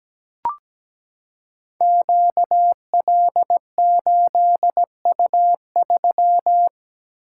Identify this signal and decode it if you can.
Morse code, heard as a keyed tone, about 17 wpm, 700 Hz. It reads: QL8U3
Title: Code Groups - 5 Letter:Numbers - 17wpm